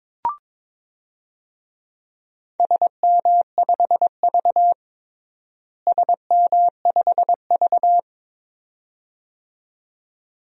Morse code, heard as a keyed tone, about 22 wpm, 700 Hz. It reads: SM5V SM5V
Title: DXCC-100 Call Signs - 4 Characters - Repeated 2x - 22wpm